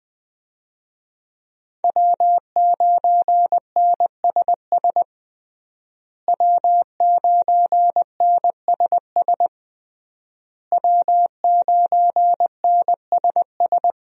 Morse code, heard as a keyed tone, about 20 wpm, 700 Hz. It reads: W9NSS W9NSS W9NSS
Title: USA Call Signs - 5 Characters - Repeated 3x - 20wpm